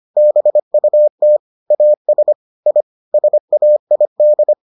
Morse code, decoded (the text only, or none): BUT AS I SAID